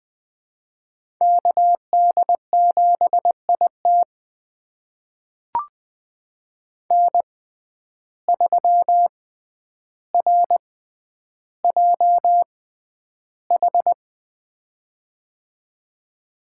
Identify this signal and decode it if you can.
Morse code, heard as a keyed tone, about 21 wpm, spaced as for 10 wpm, 700 Hz. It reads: KD7IT N3RJH
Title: USA Call Signs - 5 Characters - 6x Character Spacing - 20wpm